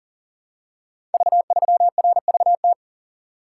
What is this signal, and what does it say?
Morse code, heard as a keyed tone, about 38 wpm, 700 Hz. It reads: V3RVT
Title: Code Groups - 5 Letter:Numbers - 40wpm